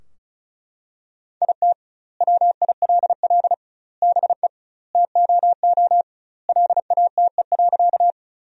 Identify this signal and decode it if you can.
Morse code, heard as a keyed tone, about 34 wpm, 700 Hz. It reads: IT WILL BE TOO LATE.
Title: 2x Word-Spacing - Sentences from Top 300 Words - 35wpm